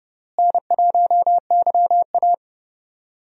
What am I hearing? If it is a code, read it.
Morse code, heard as a keyed tone, about 32 wpm, 700 Hz. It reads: N1YA
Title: USA Call Signs - 4 Characters - Repeated 3x - 30wpm